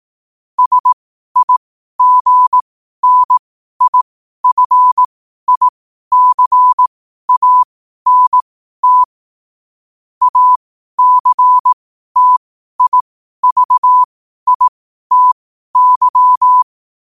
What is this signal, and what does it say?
Morse code, heard as a keyed tone, about 18 wpm, spaced as for 12 wpm, 1000 Hz. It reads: SIGNIFICANT ACTIVITY
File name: qotdfiles / 12wpm